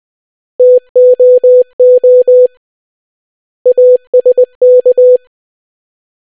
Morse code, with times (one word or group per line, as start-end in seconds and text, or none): 0.6-2.5 TOO
3.7-5.2 ASK